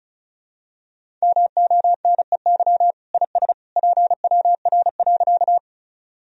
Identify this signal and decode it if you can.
Morse code, heard as a keyed tone, about 36 wpm, 700 Hz. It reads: MONEY IS PWR.
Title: Sentences from Top 500 Words and Abbreviations 35wpm